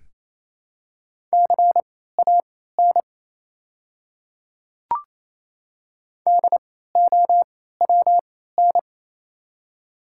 Morse code, heard as a keyed tone, about 28 wpm, spaced as for 16 wpm, 700 Hz. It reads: CAN DOWN
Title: 3x Character Spacing - Top 500 Words - 28wpm